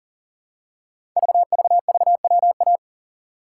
Morse code, heard as a keyed tone, about 42 wpm, 700 Hz. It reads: VVVWA
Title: Code Groups - 5 Letters - 40wpm